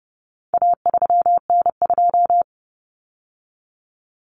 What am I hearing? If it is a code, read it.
Morse code, heard as a keyed tone, about 30 wpm, 700 Hz. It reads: A3N2
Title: Common Cut Numbers - 4-Digit 30wpm